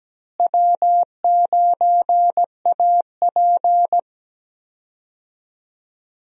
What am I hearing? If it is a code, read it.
Morse code, heard as a keyed tone, about 17 wpm, 700 Hz. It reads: W9AP
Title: USA Call Signs - 4 Characters - Repeated 3x - 17wpm